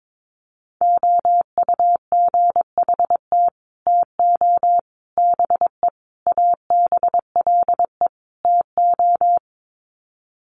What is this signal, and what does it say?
Morse code, heard as a keyed tone, about 22 wpm, 700 Hz. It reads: OUGHT TO BE ABLE TO
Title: Sets of 5 Words Encore2 22wpm